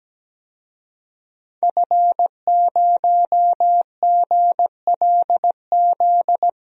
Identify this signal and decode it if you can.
Morse code, heard as a keyed tone, about 17 wpm, 700 Hz. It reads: F0GLZ
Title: DXCC-100 Call Signs - 5 Characters - 17wpm